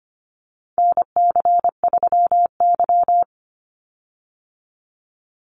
Morse code, heard as a keyed tone, about 25 wpm, 700 Hz. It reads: NC3Y
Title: Call Signs Repeated 3x - 25wpm